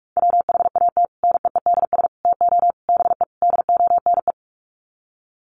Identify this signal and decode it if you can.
Morse code, heard as a keyed tone, about 44 wpm, 700 Hz. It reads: WHAT NEEDS TO BE DONE
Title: Sets of 5 Words Encore 45wpm